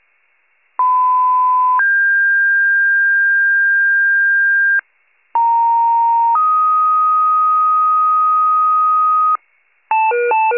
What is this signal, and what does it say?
Two-tone page, heard as a one-second tone, then a three-second tone, about 1010 then 1600 Hz, about 930 then 1250 Hz, with noise. It is heard from 0.8 to 9.4 s.
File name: 82 FIRE